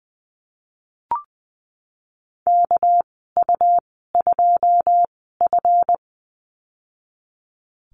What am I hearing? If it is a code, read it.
Morse code, heard as a keyed tone, about 20 wpm, spaced as for 15 wpm, 700 Hz. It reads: KU2F
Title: 238 Lesson 38g - Letter q - Call Signs 20wpm